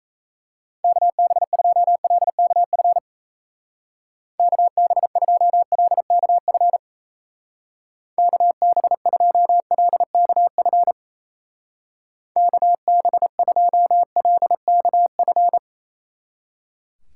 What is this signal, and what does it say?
Morse code, heard as a keyed tone, about 36 wpm, 700 Hz. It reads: KB2LKF KB2LKF KB2LKF KB2LKF